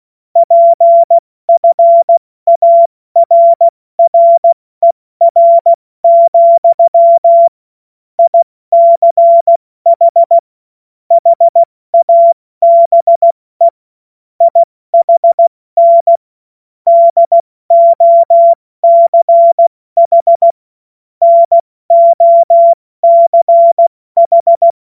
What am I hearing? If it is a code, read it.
Morse code, heard as a keyed tone, about 16 wpm, spaced as for 14 wpm, 680 Hz. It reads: PFARRER, ICH HABE IHN DOCH NOCH
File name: Morsekurs / Witze / Witz 20. September 2022 70 F80